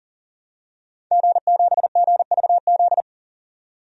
Morse code, heard as a keyed tone, about 40 wpm, 700 Hz. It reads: G7GVZ
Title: DXCC-100 Call Signs - 5 Characters - 40wpm